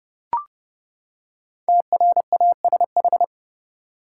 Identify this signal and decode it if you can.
Morse code, heard as a keyed tone, about 31 wpm, 700 Hz. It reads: TRASH
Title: Five Letter Words - 30wpm